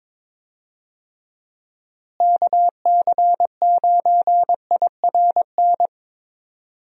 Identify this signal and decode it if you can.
Morse code, heard as a keyed tone, about 22 wpm, 700 Hz. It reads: KC9IRN
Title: USA Call Signs - 6 Characters - Repeated 3x - 22wpm